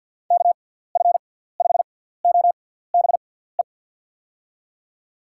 Morse code, heard as a keyed tone, about 48 wpm, spaced as for 17 wpm, 700 Hz. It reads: KF5OBE
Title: USA Call Signs - 6x Character Spacing - 50wpm